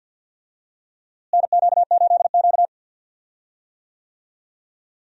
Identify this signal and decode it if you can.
Morse code, heard as a keyed tone, about 47 wpm, 700 Hz. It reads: NQ8Q
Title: USA Call Signs - 4 Characters - Repeated 3x - 50wpm